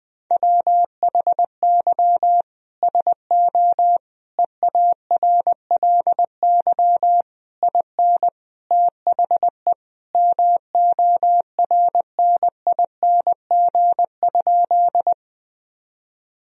Morse code, heard as a keyed tone, about 20 wpm, 700 Hz. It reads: WHY SO EARLY IN THE MORNING?
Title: Sentences from Top 1000 Words - Review Part 2 - 20wpm